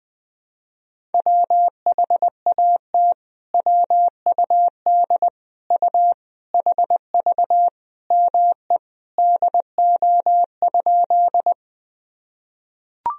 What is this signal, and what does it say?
Morse code, heard as a keyed tone, about 20 wpm, 700 Hz. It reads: WHAT WUD U HV ME DO?
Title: Sentences from Top 200 Words and Abbreviations 20wpm